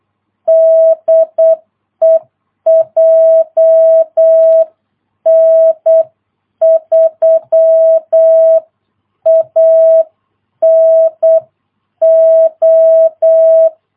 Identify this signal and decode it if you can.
Morse code, heard as a keyed tone, about 8 wpm, 650 Hz. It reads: DE JN3ANO
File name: jh8cbh / 2022cw